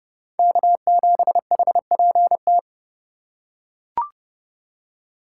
Morse code, heard as a keyed tone, about 32 wpm, 700 Hz. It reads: K7HPT
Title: K1USN SST QSO Practice - 30wpm